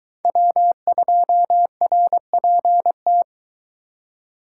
Morse code, heard as a keyed tone, about 21 wpm, 700 Hz. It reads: W2RPT